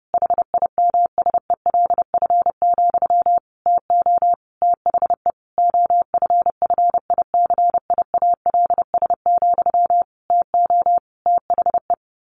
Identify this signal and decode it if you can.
Morse code, heard as a keyed tone, about 30 wpm, 700 Hz. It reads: HIMSELF, TO THE OFFICIALS, TO THE
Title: A Sound Of Thunder 30wpm